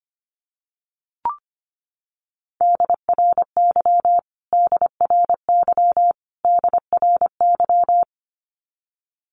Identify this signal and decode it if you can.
Morse code, heard as a keyed tone, about 25 wpm, 700 Hz. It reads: DRY DRY DRY
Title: Top 500 Words Repeated 25wpm